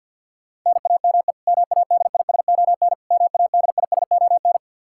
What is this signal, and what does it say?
Morse code, heard as a keyed tone, about 50 wpm, 700 Hz. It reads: NAME MADISON MADISON
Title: K1USN SST QSO Practice 50wpm